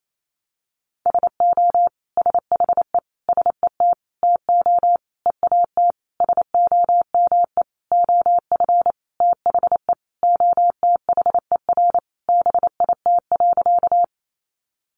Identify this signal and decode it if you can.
Morse code, heard as a keyed tone, about 28 wpm, 690 Hz. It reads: SO SHE SET TO EAT SOME OF THE OTHER BIT.
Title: Sentences from Top 900 Words 28wpm